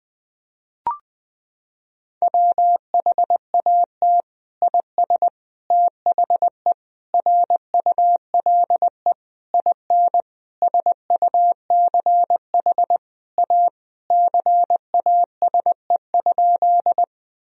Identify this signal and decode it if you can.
Morse code, heard as a keyed tone, about 20 wpm, 700 Hz. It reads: WHAT IS THE RULE IN SUCH A CASE?
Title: Sentences from Top 1000 Words - Review Part 2 - 20wpm